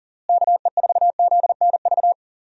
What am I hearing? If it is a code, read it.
Morse code, heard as a keyed tone, about 40 wpm, 700 Hz. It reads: KE4ZNV
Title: USA Call Signs - 6 Characters - Repeated 3x - 40wpm